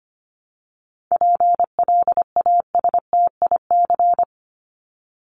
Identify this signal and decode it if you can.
Morse code, heard as a keyed tone, about 25 wpm, 700 Hz. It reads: PLASTIC
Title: Top 1001-1250 Words 25wpm